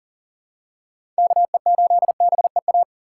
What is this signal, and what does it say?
Morse code, heard as a keyed tone, about 40 wpm, 700 Hz. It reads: KE8BEA
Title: USA Call Signs - 6 Characters - 3x Character Spacing - 40wpm